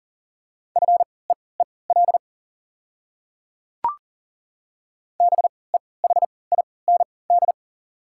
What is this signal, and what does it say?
Morse code, heard as a keyed tone, about 40 wpm, spaced as for 23 wpm, 690 Hz. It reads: FEEL BEHIND
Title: 3x Character Spacing - Top 500 Words - 40wpm